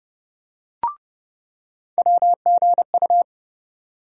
Morse code, heard as a keyed tone, about 30 wpm, 700 Hz. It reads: WGU
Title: USA Call Sign Postfixes - 3 Letters 30wpm